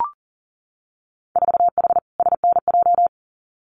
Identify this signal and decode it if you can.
Morse code, heard as a keyed tone, about 42 wpm, 700 Hz. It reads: 4H SNJ
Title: Winter Field Day Exchanges Encore 40wpm